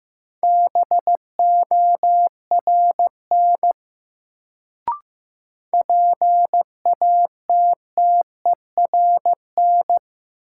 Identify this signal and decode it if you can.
Morse code, heard as a keyed tone, about 15 wpm, 700 Hz. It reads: BORN PATTERN
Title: Sentences from Top 1000 Words - New Word Drill - 15wpm